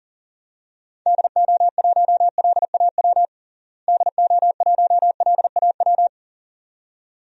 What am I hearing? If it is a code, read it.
Morse code, heard as a keyed tone, about 40 wpm, 700 Hz. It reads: DO1LAW DO1LAW
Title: DXCC-100 Call Signs - 6 Characters - Repeated 2x - 40wpm